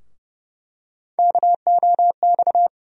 Morse code, heard as a keyed tone, about 31 wpm, 700 Hz. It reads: KOX
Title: USA Call Sign Postfixes - 3 Letters 30wpm